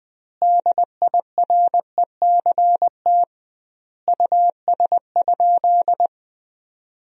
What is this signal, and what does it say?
Morse code, heard as a keyed tone, about 20 wpm, 700 Hz. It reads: DIRECT US?
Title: 2x Word-Spacing - Sentences from Top 400 Words - 20wpm